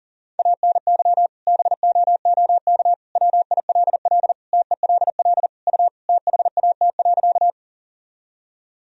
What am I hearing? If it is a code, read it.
Morse code, heard as a keyed tone, about 40 wpm, 700 Hz. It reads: ANY BOOK WILL TELL U THAT.
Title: Sentences from Top 300 Words and Abbreviations 40wpm